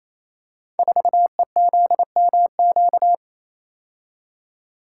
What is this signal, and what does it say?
Morse code, heard as a keyed tone, about 27 wpm, 690 Hz. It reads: 4EZMQ
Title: Code Groups - 5 Letter:Numbers - 28wpm